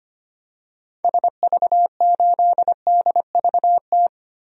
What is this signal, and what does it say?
Morse code, heard as a keyed tone, about 25 wpm, 700 Hz. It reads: SV8DVT
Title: DXCC-100 Call Signs - 6 Characters - Repeated 3x - 25wpm